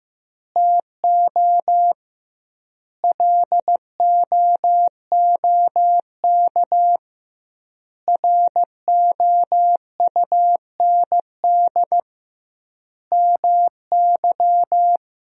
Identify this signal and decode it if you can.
Morse code, heard as a keyed tone, about 15 wpm, 700 Hz. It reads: TO LOOK ROUND MY
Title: 2x Word-Spacing - Sentences from Top 200 Words - 15wpm